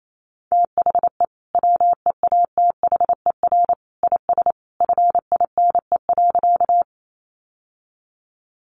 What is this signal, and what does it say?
Morse code, heard as a keyed tone, about 28 wpm, 700 Hz. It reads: THE WEATHER IS FINE.
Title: Sentences from Top 800 Words 28wpm